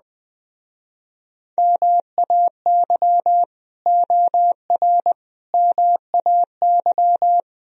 Morse code, heard as a keyed tone, about 20 wpm, 700 Hz. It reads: MAY OR MAY
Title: Speed-Racing - Sets of 5 Words 17wpm